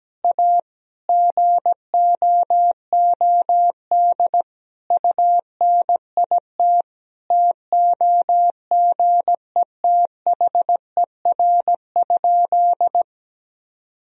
Morse code, heard as a keyed tone, about 17 wpm, 700 Hz. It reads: A GOOD UNIT TOGETHER?
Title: Sentences from Top 1000 Words - Review Part 2 - 17wpm